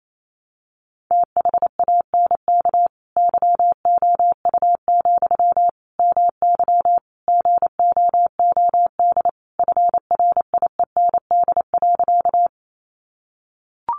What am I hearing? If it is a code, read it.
Morse code, heard as a keyed tone, about 28 wpm, 700 Hz. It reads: THANK YOU, MY GOOD FRIEND.
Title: Sentences from Top 600 Words 28wpm